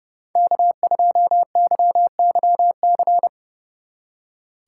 Morse code, heard as a keyed tone, about 35 wpm, 700 Hz. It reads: K2YYC
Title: Speed Racing - USA 1x3 Call Signs 30wpm